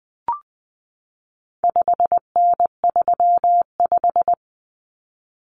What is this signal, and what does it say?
Morse code, heard as a keyed tone, about 20 wpm, 700 Hz. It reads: 5N35
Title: Common Cut Numbers - 4-Digit 20wpm